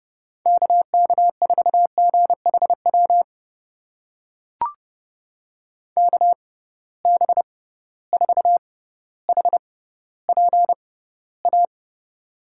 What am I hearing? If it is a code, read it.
Morse code, heard as a keyed tone, about 30 wpm, spaced as for 15 wpm, 700 Hz. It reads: KK4GHW KB4HPA
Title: USA Call Signs - 6x Character Spacing - 30wpm